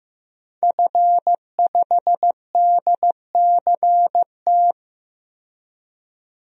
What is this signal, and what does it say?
Morse code, heard as a keyed tone, about 15 wpm, 700 Hz. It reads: F5DCT
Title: DXCC-100 Call Signs - 5 Characters - 15wpm